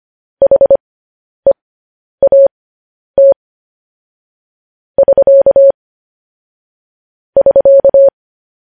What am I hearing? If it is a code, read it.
Morse code, heard as a keyed tone, about 25 wpm, spaced as for 10 wpm, 550 Hz. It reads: HEAT <SK> <SK>